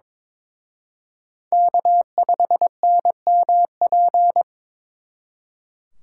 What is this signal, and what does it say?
Morse code, heard as a keyed tone, about 26 wpm, 700 Hz. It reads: K5NMP
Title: Speed Racing - USA 1x3 Call Signs 22wpm